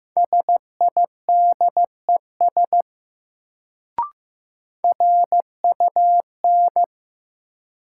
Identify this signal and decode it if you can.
Morse code, heard as a keyed tone, about 15 wpm, 700 Hz. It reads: SIDES RUN
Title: Sets of 4 Words - New Word Drill - 15wpm